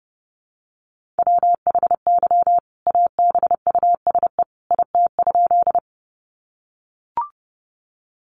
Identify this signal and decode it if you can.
Morse code, heard as a keyed tone, about 30 wpm, 700 Hz. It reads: WHY ABUSE IT?
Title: Sentences from Top 1300 Words 30wpm